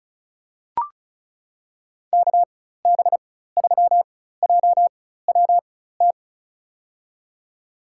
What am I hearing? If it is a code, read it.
Morse code, heard as a keyed tone, about 36 wpm, spaced as for 16 wpm, 700 Hz. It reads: KB3JWT
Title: USA Call Signs - 6 Characters - 4x Character Spacing - 35wpm